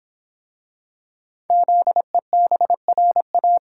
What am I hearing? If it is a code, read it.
Morse code, heard as a keyed tone, about 26 wpm, 700 Hz. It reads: ZEBRA